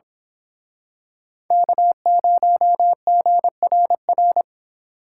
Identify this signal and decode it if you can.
Morse code, heard as a keyed tone, about 28 wpm, 700 Hz. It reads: K0GRR